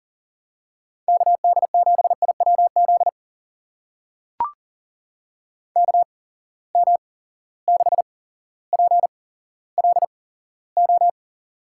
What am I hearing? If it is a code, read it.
Morse code, heard as a keyed tone, about 40 wpm, spaced as for 17 wpm, 700 Hz. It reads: KD7IWZ KM6PLO
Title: USA Call Signs - 6 Characters - 8x Character Spacing - 40wpm